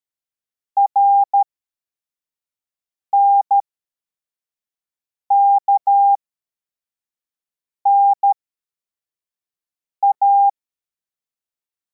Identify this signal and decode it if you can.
Morse code, heard as a keyed tone, about 13 wpm, spaced as for 4 wpm, 800 Hz. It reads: RNKNA
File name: formation / cw / Text11